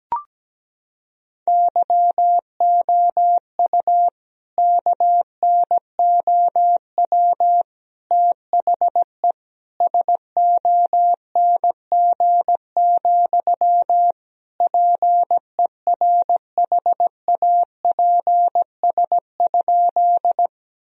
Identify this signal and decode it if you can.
Morse code, heard as a keyed tone, about 17 wpm, 700 Hz. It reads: YOU KNOW THE SONG, PERHAPS?
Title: Sentences from Top 1000 Words - Review Part 2 - 17wpm